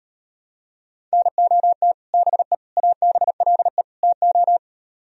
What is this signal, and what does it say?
Morse code, heard as a keyed tone, about 39 wpm, 700 Hz. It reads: NOT BE ABLE TO